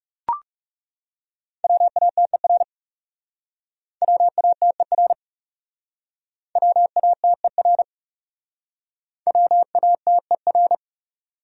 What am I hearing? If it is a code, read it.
Morse code, heard as a keyed tone, about 41 wpm, 700 Hz. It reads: WATER WATER WATER WATER